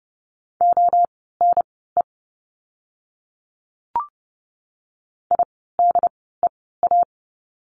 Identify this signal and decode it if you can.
Morse code, heard as a keyed tone, about 30 wpm, spaced as for 17 wpm, 700 Hz. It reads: ONE IDEA
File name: course / 30:10wpm / 108 Lesson 18f - Letter m - Words 30wpm